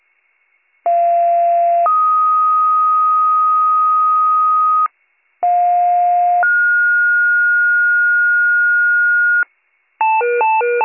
Two-tone page, heard as a one-second tone, then a three-second tone, about 690 then 1250 Hz, about 710 then 1430 Hz, with noise. It is heard from 0.8 to 9.4 s.